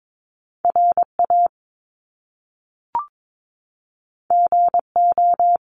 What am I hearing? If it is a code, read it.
Morse code, heard as a keyed tone, about 22 wpm, 700 Hz. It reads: RA GO
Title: Top Two Letter Combinations in Words 22wpm